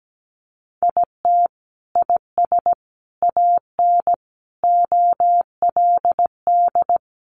Morse code, heard as a keyed tone, about 17 wpm, 700 Hz. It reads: IT IS AN OLD
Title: Sentences from Top 1500 Words 17wpm